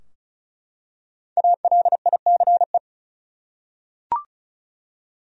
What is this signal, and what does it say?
Morse code, heard as a keyed tone, about 36 wpm, 700 Hz. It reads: ALICE
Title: USA Common Names - 35wpm